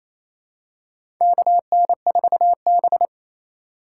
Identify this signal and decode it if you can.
Morse code, heard as a keyed tone, about 28 wpm, 700 Hz. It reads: KN4B
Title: USA Call Signs - 3x Character Spacing - 28wpm